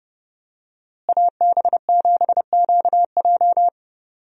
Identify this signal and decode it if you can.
Morse code, heard as a keyed tone, about 32 wpm, 700 Hz. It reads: AB7QJ
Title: USA Call Signs - 5 Characters - Repeated 2x - 30wpm